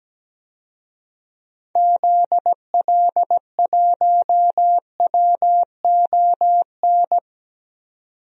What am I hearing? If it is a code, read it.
Morse code, heard as a keyed tone, about 17 wpm, 700 Hz. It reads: ZL1WON
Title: DXCC-100 Call Signs - 6 Characters - 17wpm